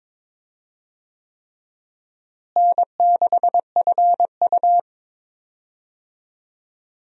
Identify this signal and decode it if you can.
Morse code, heard as a keyed tone, about 22 wpm, 700 Hz. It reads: N6FU
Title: USA Call Signs - 4 Characters - Repeated 3x - 22wpm